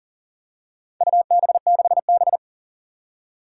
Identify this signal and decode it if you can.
Morse code, heard as a keyed tone, about 40 wpm, 700 Hz. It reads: UB6B
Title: Four Challenging Letter-Number Combinations 40wpm